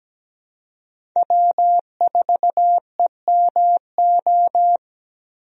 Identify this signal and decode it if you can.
Morse code, heard as a keyed tone, about 17 wpm, 700 Hz. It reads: W4EMO
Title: USA Call Signs - 5 Characters - Repeated 2x - 17wpm